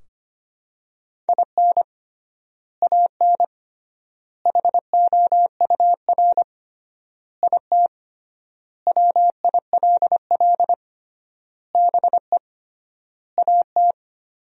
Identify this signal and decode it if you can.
Morse code, heard as a keyed tone, about 25 wpm, 700 Hz. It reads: IN AN HOUR IT WILL BE AT
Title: 3x Word-Spacing - Sentences from Top 400 Words - 25wpm